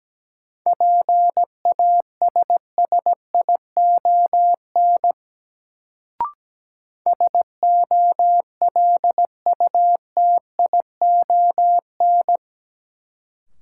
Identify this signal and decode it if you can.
Morse code, heard as a keyed tone, about 17 wpm, 700 Hz. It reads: PASSION SOLUTION
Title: Sentences from Top 1300 Words - New Word Drill - 17wpm